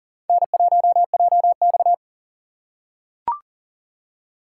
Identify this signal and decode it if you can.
Morse code, heard as a keyed tone, about 38 wpm, 700 Hz. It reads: N1JX
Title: USA Call Signs - 4 Characters - 2x Character Spacing - 40wpm